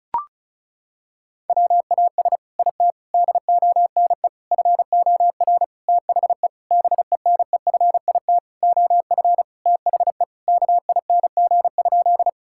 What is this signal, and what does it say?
Morse code, heard as a keyed tone, about 36 wpm, 700 Hz. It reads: WAS IT DONE FOR THE BENEFIT OF THE KING?
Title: Sentences from Top 1000 Words - Review Part 6 - 35wpm